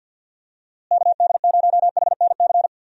Morse code, heard as a keyed tone, about 46 wpm, 700 Hz. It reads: KD0HNC
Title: USA Call Signs - 6 Characters - Repeated 2x - 50wpm